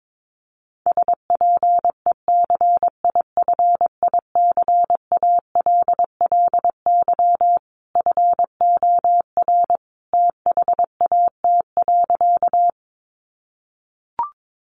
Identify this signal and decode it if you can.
Morse code, heard as a keyed tone, about 22 wpm, 700 Hz. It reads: SPECIFICALLY FOR THAT.
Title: Sentences from Top 1100 Words 22wpm